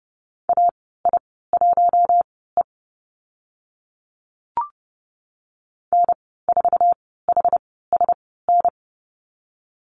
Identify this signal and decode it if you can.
Morse code, heard as a keyed tone, about 30 wpm, spaced as for 17 wpm, 700 Hz. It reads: AI1E N4HSN
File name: Lesson 14g - Number 2 - Call Signs 30wpm